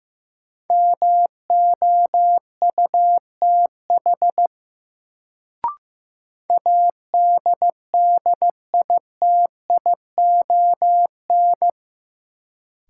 Morse code, heard as a keyed tone, about 15 wpm, 700 Hz. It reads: MOUTH ADDITION
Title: Sentences from Top 1500 Words - New Word Drill Alt - 15wpm